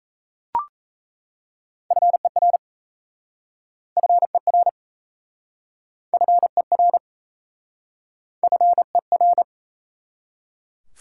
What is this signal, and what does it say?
Morse code, heard as a keyed tone, about 38 wpm, 700 Hz. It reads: FER FER FER FER